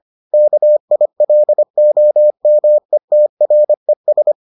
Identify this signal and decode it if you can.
Morse code, heard as a keyed tone, about 25 wpm, 600 Hz. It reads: KILOMETRES